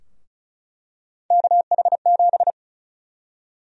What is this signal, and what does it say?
Morse code, heard as a keyed tone, about 36 wpm, 700 Hz. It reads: KH7
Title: USA Call Sign Prefixes - Repeated 2x - 35wpm